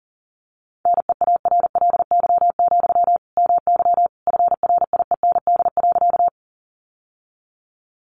Morse code, heard as a keyed tone, about 40 wpm, 700 Hz. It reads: NEARLY, MY FRIEND.
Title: Sentences from Top 1300 Words 40wpm